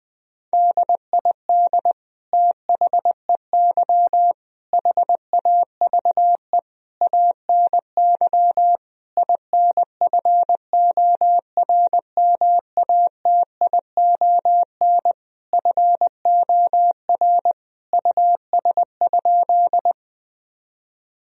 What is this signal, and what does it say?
Morse code, heard as a keyed tone, about 20 wpm, 700 Hz. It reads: DID THEY HAVE ANY INFORMATION FOR US?
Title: Sentences from Top 1000 Words - Review Part 3 - 20wpm